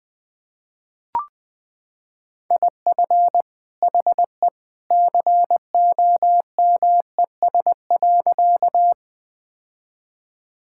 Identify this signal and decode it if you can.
Morse code, heard as a keyed tone, about 20 wpm, 700 Hz. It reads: IF HE COMES.
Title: Sentences with 2-Word Phrases - 20wpm